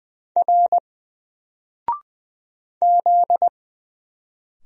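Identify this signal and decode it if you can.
Morse code, heard as a keyed tone, about 20 wpm, 700 Hz. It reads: R Z
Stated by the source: Single letter and number Morse drill at 20 wpm, from N4LCD.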